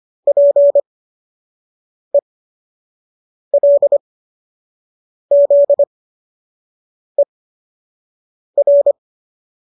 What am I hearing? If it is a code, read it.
Morse code, heard as a keyed tone, about 25 wpm, spaced as for 6 wpm, 570 Hz. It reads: PELZER